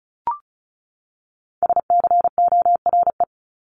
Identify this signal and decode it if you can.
Morse code, heard as a keyed tone, about 34 wpm, 700 Hz. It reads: SCORE
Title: Top 1501-1750 Words 35wpm